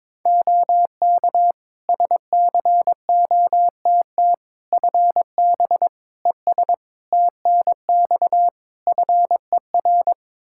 Morse code, heard as a keyed tone, about 22 wpm, 700 Hz. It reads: OK SCOTT FB ES TNX FER
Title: Practice QSO - Exchange 2 - Info - 22wpm